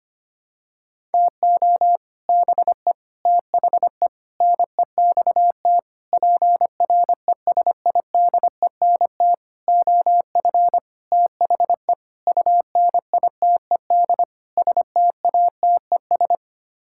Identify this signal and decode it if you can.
Morse code, heard as a keyed tone, about 25 wpm, 700 Hz. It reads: TO BE THE NEXT PRESIDENT OF THE UNITED STATES
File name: Sets of 9 Words - 25wpm